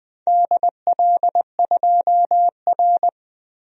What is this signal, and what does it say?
Morse code, heard as a keyed tone, about 20 wpm, 700 Hz. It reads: DL2R
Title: DXCC-100 Call Signs - 4 Characters - Repeated 3x - 20wpm